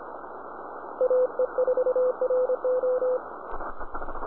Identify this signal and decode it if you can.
Morse code, heard as a keyed tone, about 26 wpm, 510 Hz. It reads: AE4RO